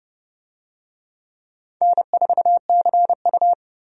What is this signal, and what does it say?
Morse code, heard as a keyed tone, about 30 wpm, 700 Hz. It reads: N4CU
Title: USA Call Signs - 4 Characters - 30wpm